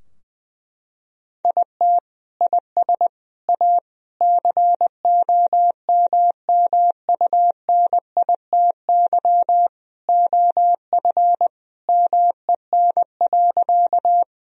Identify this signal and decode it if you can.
Morse code, heard as a keyed tone, about 21 wpm, 700 Hz. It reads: IT IS A COMMUNITY OF MEN.
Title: Sentences from Top 1000 Words - Review Part 4 - 20wpm